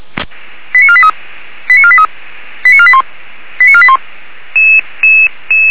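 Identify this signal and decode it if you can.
Five-tone selective call (ZVEI), heard as quick runs of five tones, tones of about 70 ms. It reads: 89483 89483 89481 89481
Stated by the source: ZVEI five-tone selective call received on SDR.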